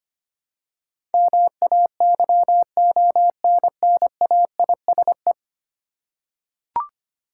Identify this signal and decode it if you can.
Morse code, heard as a keyed tone, about 25 wpm, 700 Hz. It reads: MAYONNAISE
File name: practice / 25wpm / Easy Words with Hints - 25wpm